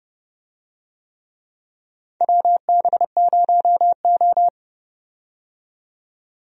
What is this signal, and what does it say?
Morse code, heard as a keyed tone, about 30 wpm, 700 Hz. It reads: WB0O
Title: USA Call Signs - 4 Characters - 30wpm